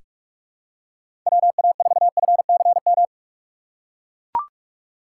Morse code, heard as a keyed tone, about 42 wpm, 700 Hz. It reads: WA4FCM
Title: USA Call Signs - 6 Characters - 4x Character Spacing - 45wpm